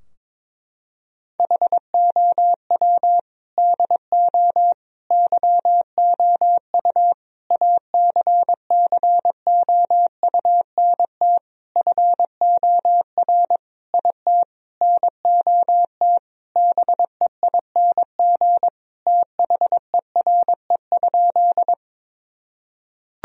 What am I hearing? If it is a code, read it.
Morse code, heard as a keyed tone, about 22 wpm, 700 Hz. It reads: HOW DO YOU ACCOUNT FOR IT NOT BEING THERE?
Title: Sentences from Top 1000 Words - Review Part 3 - 22wpm